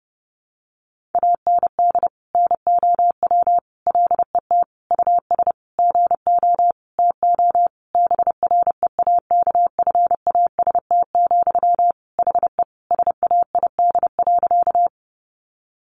Morse code, heard as a keyed tone, about 30 wpm, 700 Hz. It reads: AND NOW LET US GO TO BREAKFAST, HE SAID.
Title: Sentences from Top 1500 Words 30wpm